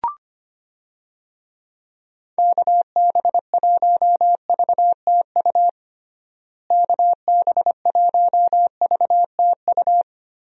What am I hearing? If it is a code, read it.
Morse code, heard as a keyed tone, about 25 wpm, 700 Hz. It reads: KB1VTU KB1VTU
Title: USA Call Signs - 6 Characters - Repeated 3x - 25wpm